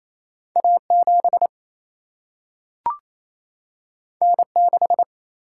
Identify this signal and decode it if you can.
Morse code, heard as a keyed tone, about 28 wpm, 700 Hz. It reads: A7 N6
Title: Single-Letter Single-Number 28wpm